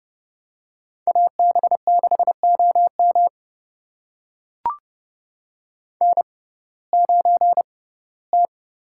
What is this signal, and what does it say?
Morse code, heard as a keyed tone, about 30 wpm, spaced as for 18 wpm, 700 Hz. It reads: AB6OM N9T
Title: USA Call Signs - 5 Characters - 6x Character Spacing - 30wpm